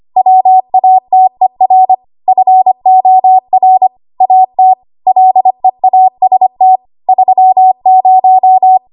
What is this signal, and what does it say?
Morse code, heard as a keyed tone, about 25 wpm, 750 Hz. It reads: WATER FOR AT LEAST 30
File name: Archive / 25 WPM / 25WPM